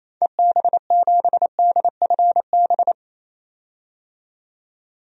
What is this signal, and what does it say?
Morse code, heard as a keyed tone, about 28 wpm, 700 Hz. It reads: EB7DFB
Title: DXCC-100 Call Signs - 6 Characters - 28wpm